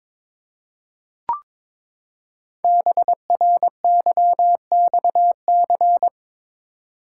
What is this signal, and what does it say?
Morse code, heard as a keyed tone, about 23 wpm, 700 Hz. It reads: BRYXC
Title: Code Groups - 5 Letters - 22wpm